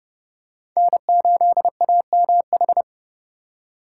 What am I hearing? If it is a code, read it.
Morse code, heard as a keyed tone, about 30 wpm, 700 Hz. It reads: N8AMH
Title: USA Call Signs - 3x Character Spacing - 30wpm